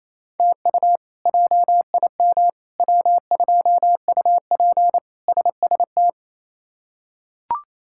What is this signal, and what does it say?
Morse code, heard as a keyed tone, about 27 wpm, 700 Hz. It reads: TU JIM W2UP SST
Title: K1USN SST QSO Practice - 28wpm